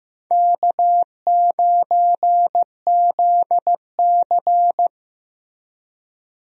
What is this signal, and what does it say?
Morse code, heard as a keyed tone, about 15 wpm, 700 Hz. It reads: K9ZC
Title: USA Call Signs - 4 Characters - Repeated 3x - 15wpm